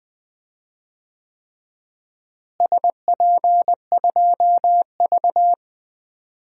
Morse code, heard as a keyed tone, about 20 wpm, 700 Hz. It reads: SP2V
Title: DXCC-100 Call Signs - 4 Characters - Repeated 3x - 20wpm